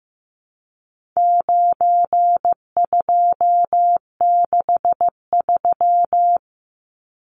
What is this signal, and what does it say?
Morse code, heard as a keyed tone, about 15 wpm, 700 Hz. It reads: 9263
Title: Numbers - 4-Digit 15wpm